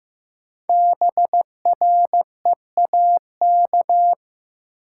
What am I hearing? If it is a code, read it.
Morse code, heard as a keyed tone, about 15 wpm, 700 Hz. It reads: BREAK
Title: Sets of 4 Words - New Word Drill - 15wpm